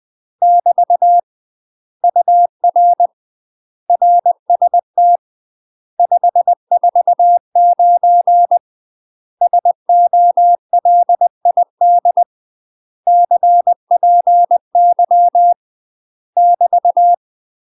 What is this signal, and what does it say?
Morse code, heard as a keyed tone, about 20 wpm, 700 Hz. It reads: <BT> UR RST 549 SOLID CPY <BT>
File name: UK MASTER QSOs TRAINING / 20WPM / UK MASTER QSOs 0031